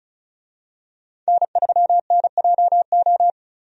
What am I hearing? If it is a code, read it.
Morse code, heard as a keyed tone, about 34 wpm, 700 Hz. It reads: N3NJO
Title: USA Call Signs - 5 Characters - 35wpm